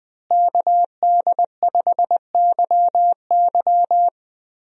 Morse code, heard as a keyed tone, about 20 wpm, 700 Hz. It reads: KD5YY
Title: USA Call Signs - 5 Characters - Repeated 2x - 20wpm